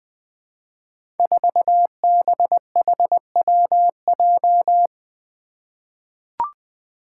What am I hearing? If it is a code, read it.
Morse code, heard as a keyed tone, about 20 wpm, 700 Hz. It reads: 4BHWJ
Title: Five Challenging Letter-Number Combinations 20wpm